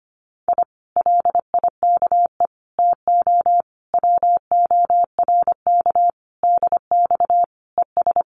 Morse code, heard as a keyed tone, about 25 wpm, 700 Hz. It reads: I LIKE TO WORK DX ES
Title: Common QSO Phrases Encore 25wpm